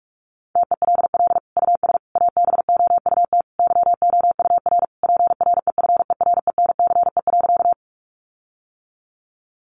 Morse code, heard as a keyed tone, about 44 wpm, 700 Hz. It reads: TELL US ABOUT YOUR PREFERENCE.
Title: Sentences from Top 1500 Words 45wpm